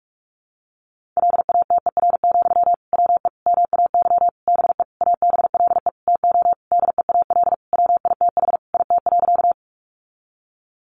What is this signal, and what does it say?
Morse code, heard as a keyed tone, about 44 wpm, 700 Hz. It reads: LATER, WE MAY BE ABLE TO DEAL WITH IT.
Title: Sentences from Top 800 Words 45wpm